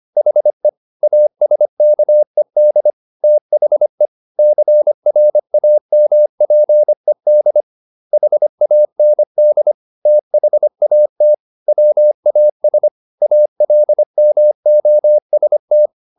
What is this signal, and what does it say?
Morse code, heard as a keyed tone, about 25 wpm, 600 Hz. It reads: HE ASKED THE CRAMPED HAND THAT WAS ALMOST
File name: the old man and the sea part 1 25wpm